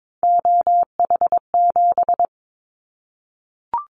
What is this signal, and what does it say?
Morse code, heard as a keyed tone, about 22 wpm, 700 Hz. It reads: OH7
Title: Callsign Prefixes for Top 25 DXCC - Repeated 3x - 22wpm